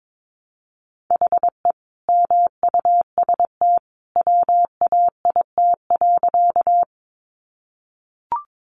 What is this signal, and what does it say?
Morse code, heard as a keyed tone, about 22 wpm, 700 Hz. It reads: HE MUST WAIT.
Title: Sentences from Top 500 Words 22wpm